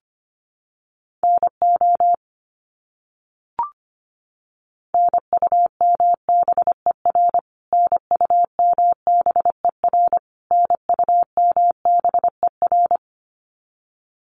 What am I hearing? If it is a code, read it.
Morse code, heard as a keyed tone, about 25 wpm, 700 Hz. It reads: NO NUMBER NUMBER NUMBER
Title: Top 300 Words Repeated 25wpm